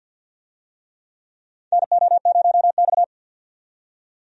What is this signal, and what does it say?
Morse code, heard as a keyed tone, about 50 wpm, 700 Hz. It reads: NO0X
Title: USA Call Signs - 4 Characters - 50wpm